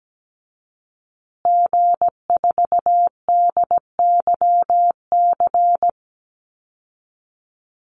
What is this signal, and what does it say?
Morse code, heard as a keyed tone, about 17 wpm, 700 Hz. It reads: G4DYC
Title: Call Signs Repeated 2x - 17wpm